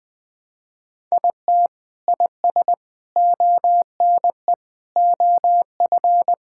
Morse code, heard as a keyed tone, about 21 wpm, 700 Hz. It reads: IT IS ONE OF
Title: Sentences with 4-Word Phrases - 20wpm